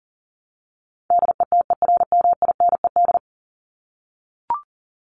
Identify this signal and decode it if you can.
Morse code, heard as a keyed tone, about 40 wpm, 700 Hz. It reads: DETERMINED
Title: ED Suffix - Top 100 Words with Base Words 40wpm